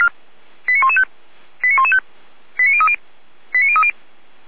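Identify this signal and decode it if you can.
Five-tone selective call (ZVEI), heard as quick runs of five tones, tones of about 70 ms. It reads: 89115 89115 89030 89030
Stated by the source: ZVEI five-tone call received by SDR.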